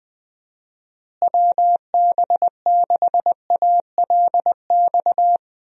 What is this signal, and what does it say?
Morse code, heard as a keyed tone, about 20 wpm, 700 Hz. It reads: WB6ALX
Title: USA Call Signs - 6 Characters - Repeated 3x - 20wpm